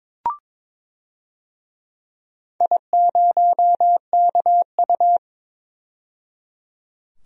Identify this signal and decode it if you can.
Morse code, heard as a keyed tone, about 22 wpm, 700 Hz. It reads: I0KU
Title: DXCC-100 Call Signs - 4 Characters - 22wpm